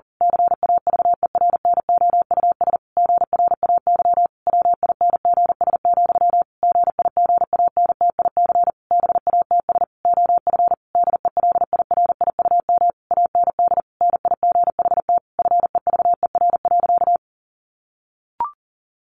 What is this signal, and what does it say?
Morse code, heard as a keyed tone, about 40 wpm, 700 Hz. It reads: CAVERNOUS GRAY WINGS, GIGANTIC BATS OF DELIRIUM AND NIGHT FEVER.
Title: A Sound Of Thunder 40wpm